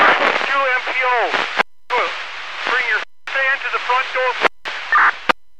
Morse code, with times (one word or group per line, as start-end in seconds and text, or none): none